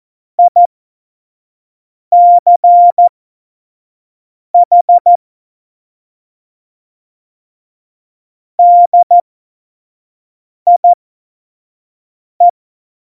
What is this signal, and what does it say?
Morse code, heard as a keyed tone, about 14 wpm, spaced as for 5 wpm, 700 Hz. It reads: ICH DIE
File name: Morsekurs / Witze / Witz 18 August 2020, 25 F70